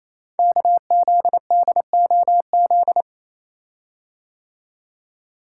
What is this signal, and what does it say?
Morse code, heard as a keyed tone, about 28 wpm, 700 Hz. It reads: KZDOZ
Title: Code Groups - 5 Letter:Numbers - 28wpm